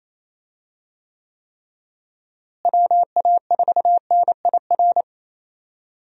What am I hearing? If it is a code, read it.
Morse code, heard as a keyed tone, about 27 wpm, 700 Hz. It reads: WA4NIR
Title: USA Call Signs - 6 Characters - Repeated 3x - 28wpm